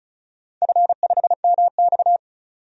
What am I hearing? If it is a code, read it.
Morse code, heard as a keyed tone, about 36 wpm, 700 Hz. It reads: F5MX
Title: DXCC-100 Call Signs - 4 Characters - 35wpm